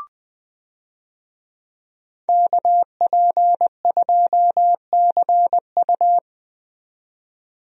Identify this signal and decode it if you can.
Morse code, heard as a keyed tone, about 20 wpm, 700 Hz. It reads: KP2CU
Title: DXCC-100 Call Signs - 5 Characters - 20wpm